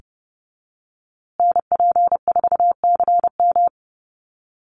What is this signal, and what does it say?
Morse code, heard as a keyed tone, about 30 wpm, 700 Hz. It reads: NP4CM
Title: USA Call Signs Repeated 3x 30wpm